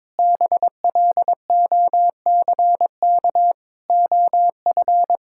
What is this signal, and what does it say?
Morse code, heard as a keyed tone, about 22 wpm, 700 Hz. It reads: BLOCK OF
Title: Sentences from Top 1600 Words 22wpm